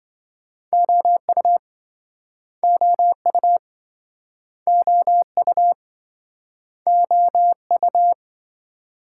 Morse code, heard as a keyed tone, about 26 wpm, 700 Hz. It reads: OU OU OU OU